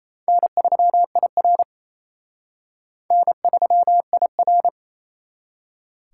Morse code, heard as a keyed tone, about 32 wpm, 700 Hz. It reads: N3IR N3IR
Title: Speed Racing - USA 1x2 Call Signs 28wpm